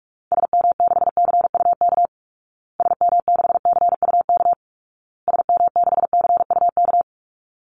Morse code, heard as a keyed tone, about 44 wpm, 700 Hz. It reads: SM6CUK SM6CUK SM6CUK
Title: Call Signs Repeated 2x - 45wpm